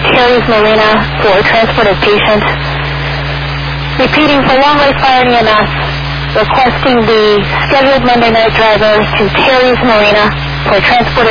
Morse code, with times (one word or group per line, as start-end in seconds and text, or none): none